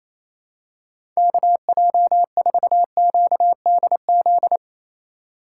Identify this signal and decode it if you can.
Morse code, heard as a keyed tone, about 28 wpm, 700 Hz. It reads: KJ4QDZ
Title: USA Call Signs - 6 Characters - 3x Character Spacing - 28wpm